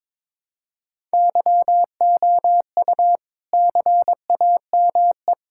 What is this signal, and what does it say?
Morse code, heard as a keyed tone, about 22 wpm, 700 Hz. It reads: YOU CAME
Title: Sentences from Top 1000 Words - Review Part 6 - 22wpm